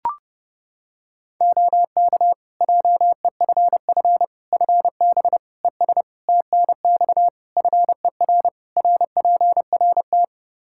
Morse code, heard as a keyed tone, about 30 wpm, 700 Hz. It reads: OK JEFF FB ES TNX FER RPRT
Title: K1USN SST QSO Practice 30wpm